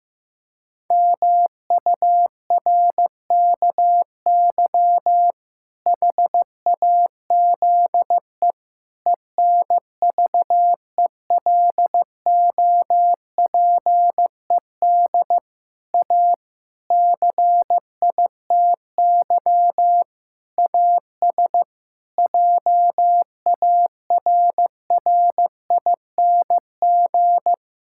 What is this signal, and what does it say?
Morse code, heard as a keyed tone, about 14 wpm, 700 Hz. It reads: MURKY HAZE ENVELOPED A CITY AS JARRING
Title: Pangram Challenge v2 - 15wpm